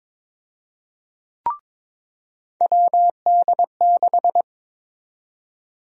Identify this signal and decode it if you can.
Morse code, heard as a keyed tone, about 22 wpm, 700 Hz. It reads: WD6
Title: USA Call Sign Prefixes - 22wpm